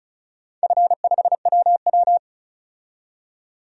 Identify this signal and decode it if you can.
Morse code, heard as a keyed tone, about 36 wpm, 700 Hz. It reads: F5WW
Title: DXCC-100 Call Signs - 4 Characters - 35wpm